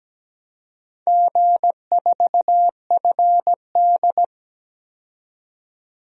Morse code, heard as a keyed tone, about 17 wpm, 700 Hz. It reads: G4FD
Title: DXCC-100 Call Signs - 4 Characters - 17wpm